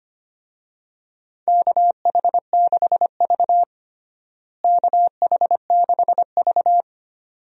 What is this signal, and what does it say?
Morse code, heard as a keyed tone, about 25 wpm, 700 Hz. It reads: KH6V KH6V
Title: DXCC-100 Call Signs - 4 Characters - Repeated 2x - 25wpm